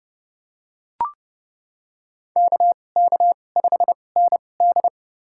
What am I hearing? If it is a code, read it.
Morse code, heard as a keyed tone, about 29 wpm, spaced as for 21 wpm, 700 Hz. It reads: KK5ND
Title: USA Call Signs - 5 Characters - 2x Character Spacing - 30wpm